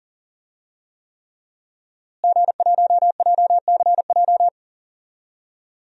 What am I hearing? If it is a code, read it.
Morse code, heard as a keyed tone, about 40 wpm, 700 Hz. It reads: G1JCJ
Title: DXCC-100 Call Signs - 5 Characters - Repeated 3x - 40wpm